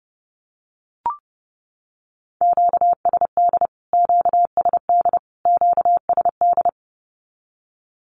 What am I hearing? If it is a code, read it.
Morse code, heard as a keyed tone, about 30 wpm, 700 Hz. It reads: QSD QSD QSD
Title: Q-Codes Repeated 3x 30wpm